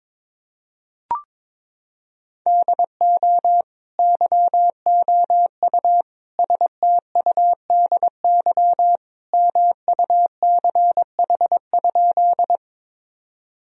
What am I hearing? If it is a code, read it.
Morse code, heard as a keyed tone, about 22 wpm, 700 Hz. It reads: DO YOU STUDY MUCH?
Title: Sentences from Top 1000 Words - Review Part 1 - 22wpm